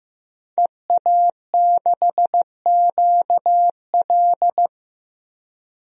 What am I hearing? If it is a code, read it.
Morse code, heard as a keyed tone, about 15 wpm, 700 Hz. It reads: EA6QL
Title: DXCC-100 Call Signs - 5 Characters - Repeated 2x - 15wpm